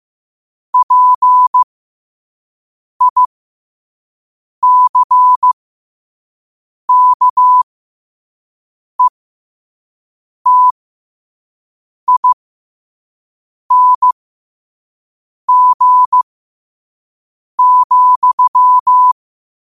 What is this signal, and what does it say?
Morse code, heard as a keyed tone, about 15 wpm, spaced as for 5 wpm, 1000 Hz. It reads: PICKETING,
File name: qotdfiles / 5wpm